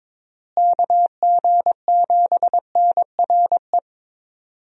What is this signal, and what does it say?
Morse code, heard as a keyed tone, about 22 wpm, 700 Hz. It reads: KG7NRE
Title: USA Call Signs - 6 Characters - Repeated 3x - 22wpm